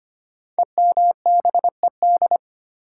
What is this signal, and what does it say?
Morse code, heard as a keyed tone, about 25 wpm, 700 Hz. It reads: EMBED
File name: 5 Letter Words - First 3 Letters Given - 25wpm